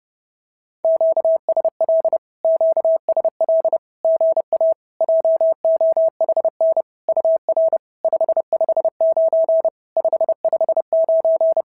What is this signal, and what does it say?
Morse code, heard as a keyed tone, about 31 wpm, 650 Hz. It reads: QSL QSL GA JOHN UR 559 559
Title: SOTA QSO Practice 30wpm